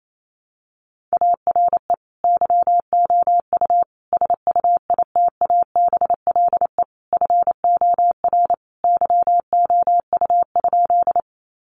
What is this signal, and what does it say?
Morse code, heard as a keyed tone, about 28 wpm, 700 Hz. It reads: ARE YOU SUITABLE FOR YOU?
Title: Sentences from Top 1400 Words 28wpm